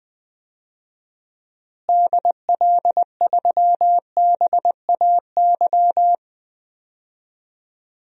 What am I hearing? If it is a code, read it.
Morse code, heard as a keyed tone, about 20 wpm, 700 Hz. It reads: DL3BAY
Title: DXCC-100 Call Signs - 6 Characters - Repeated 3x - 20wpm